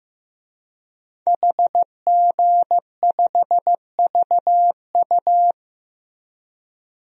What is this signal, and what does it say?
Morse code, heard as a keyed tone, about 15 wpm, 700 Hz. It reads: HG5VU
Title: DXCC-100 Call Signs - 5 Characters - Repeated 3x - 15wpm